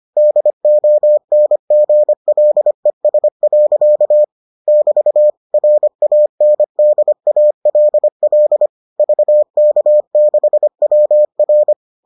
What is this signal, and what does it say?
Morse code, heard as a keyed tone, about 25 wpm, 600 Hz. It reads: DONGLES. <BT> RANDALL VK6WR